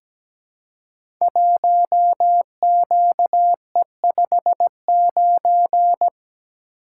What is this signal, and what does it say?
Morse code, heard as a keyed tone, about 17 wpm, 700 Hz. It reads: 1QE59
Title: Code Groups - 5 Letter:Numbers - 17wpm